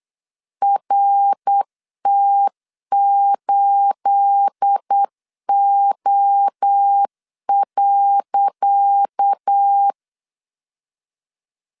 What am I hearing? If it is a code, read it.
Morse code, heard as a keyed tone, about 8 wpm, 800 Hz. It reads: RT8O.
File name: LEZIONI / 10 SIGNS / 10C DOT 8wpm 3649